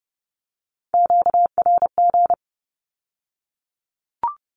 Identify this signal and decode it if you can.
Morse code, heard as a keyed tone, about 30 wpm, 690 Hz. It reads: QRG
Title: Q-Codes with Spoken Meaning - 30wpm